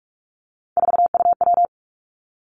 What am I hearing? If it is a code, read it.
Morse code, heard as a keyed tone, about 46 wpm, 700 Hz. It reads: 4UW
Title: Three Challenging Letter-Number Combinations 45wpm